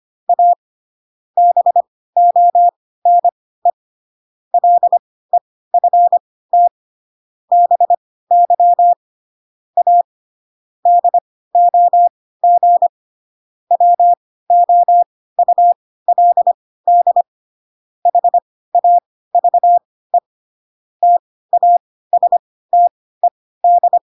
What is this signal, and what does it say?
Morse code, heard as a keyed tone, about 25 wpm, spaced as for 16 wpm, 700 Hz. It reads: A BONE LEFT BY A DOG WOULD HAVE TASTED
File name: Pinocchio / Pinocchio ch5 10 25 ik1hge